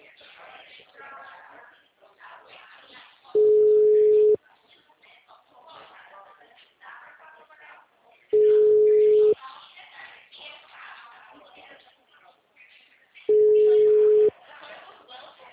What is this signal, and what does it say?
Ringback tone (Europe, 425 Hz), with speech.
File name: out- -2011- - - .66672